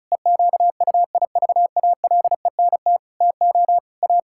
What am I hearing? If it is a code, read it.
Morse code, heard as a keyed tone, about 36 wpm, 700 Hz. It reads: EQUIVALENT TO A
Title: Sentences from Top 1600 Words 35wpm